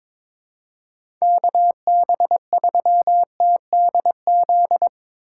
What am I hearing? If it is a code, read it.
Morse code, heard as a keyed tone, about 22 wpm, 700 Hz. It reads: KB3TDZ
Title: USA Call Signs - 6 Characters - 8x Character Spacing - 22wpm